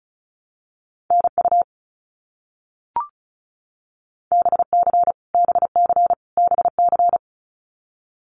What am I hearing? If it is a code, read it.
Morse code, heard as a keyed tone, about 34 wpm, 700 Hz. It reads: NU BC BC BC
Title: Canadian Provinces and Territories - Repeated 3x - 35wpm